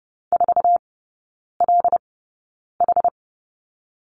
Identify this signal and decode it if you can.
Morse code, heard as a keyed tone, about 30 wpm, spaced as for 9 wpm, 700 Hz. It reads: 4LH
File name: 081 Lesson 14e - Number 2 - Three Characters 30wpm